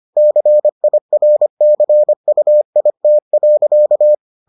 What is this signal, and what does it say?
Morse code, heard as a keyed tone, about 25 wpm, 600 Hz. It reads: CIRCUIT.